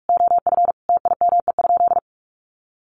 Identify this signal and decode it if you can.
Morse code, heard as a keyed tone, about 44 wpm, 700 Hz. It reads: OF TIME?
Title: Sentences from Top 1200 Words 45wpm